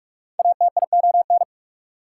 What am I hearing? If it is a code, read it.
Morse code, heard as a keyed tone, about 44 wpm, 690 Hz. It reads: ATION
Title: Prefixes and Suffixes 45wpm